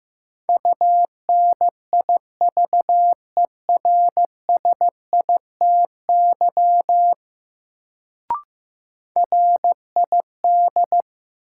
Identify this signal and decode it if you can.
Morse code, heard as a keyed tone, about 15 wpm, 700 Hz. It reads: UNIVERSITY RID
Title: Sets of 4 Words - New Word Drill - 15wpm